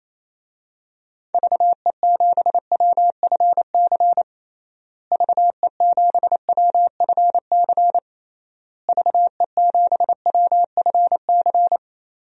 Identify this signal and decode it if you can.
Morse code, heard as a keyed tone, about 27 wpm, 700 Hz. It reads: VE7WFC VE7WFC VE7WFC
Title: DXCC-100 Call Signs - 6 Characters - Repeated 3x - 28wpm